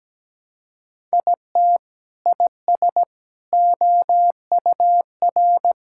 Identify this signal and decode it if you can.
Morse code, heard as a keyed tone, about 17 wpm, 700 Hz. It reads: IT IS OUR
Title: Sentences from Top 1000 Words - Review Part 6 - 17wpm